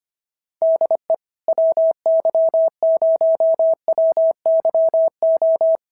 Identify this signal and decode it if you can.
Morse code, heard as a keyed tone, about 24 wpm, 650 Hz. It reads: DE WY0WYO
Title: SOTA QSO Practice 25wpm